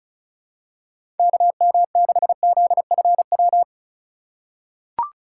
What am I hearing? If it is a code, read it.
Morse code, heard as a keyed tone, about 36 wpm, 700 Hz. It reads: KM6ZFW
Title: USA Call Signs - 6 Characters - 3x Character Spacing - 35wpm